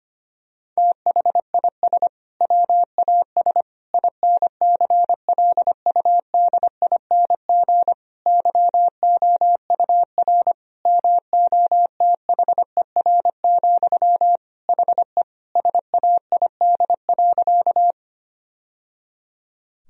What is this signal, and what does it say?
Morse code, heard as a keyed tone, about 25 wpm, 700 Hz. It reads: THIS WAS INCLUDING YOUR MOTHER, HE SAID.
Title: Sentences from Top 1000 Words - Review Part 4 - 25wpm